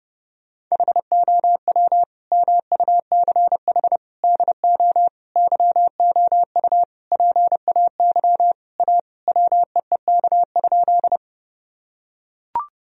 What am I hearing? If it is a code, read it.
Morse code, heard as a keyed tone, about 28 wpm, 700 Hz. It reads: HOW MUCH DO YOU PAY A WEEK?
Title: Sentences from Top 1000 Words - Review Part 2 - 30wpm